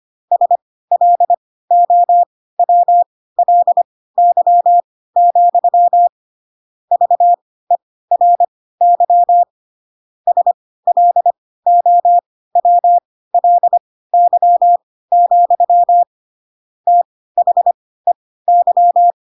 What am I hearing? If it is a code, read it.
Morse code, heard as a keyed tone, about 25 wpm, spaced as for 16 wpm, 700 Hz. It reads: SLOWLY, VERY SLOWLY, THEY
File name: Pinocchio / Pinocchio ch6 10 25 ik1hge